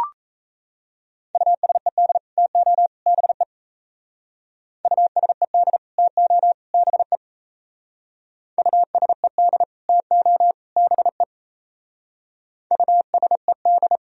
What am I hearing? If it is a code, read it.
Morse code, heard as a keyed tone, about 37 wpm, 700 Hz. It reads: USED TO BE USED TO BE USED TO BE USED